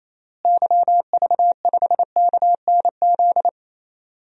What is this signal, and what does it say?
Morse code, heard as a keyed tone, about 28 wpm, 700 Hz. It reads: YV5KNZ
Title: DXCC-100 Call Signs - 6 Characters - Repeated 3x - 28wpm